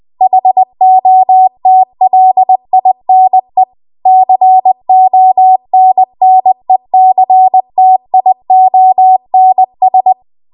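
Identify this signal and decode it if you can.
Morse code, heard as a keyed tone, about 20 wpm, 750 Hz. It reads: HOTLINE CONNECTIONS
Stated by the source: Morse code practice at 20 WPM.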